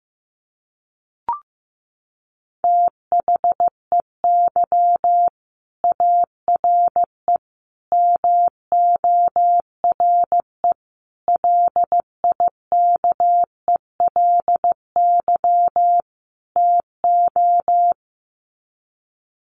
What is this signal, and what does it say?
Morse code, heard as a keyed tone, about 15 wpm, 700 Hz. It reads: THEY ARE MORE LIKELY TO
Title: Sets of 5 Words Encore2 15wpm